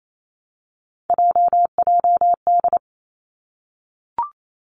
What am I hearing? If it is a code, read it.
Morse code, heard as a keyed tone, about 28 wpm, 700 Hz. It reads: JJD
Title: Three Challenging Letter-Number Combinations 28wpm